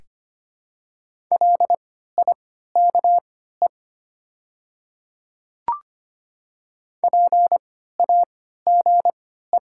Morse code, heard as a keyed tone, about 25 wpm, spaced as for 14 wpm, 700 Hz. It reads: LIKE PAGE
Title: 3x Character Spacing - Top 300 Words - 25wpm